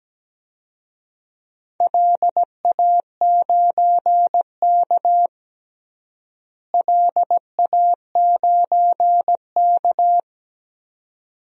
Morse code, heard as a keyed tone, about 17 wpm, 700 Hz. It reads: LA9K LA9K
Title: DXCC-100 Call Signs - 4 Characters - Repeated 2x - 17wpm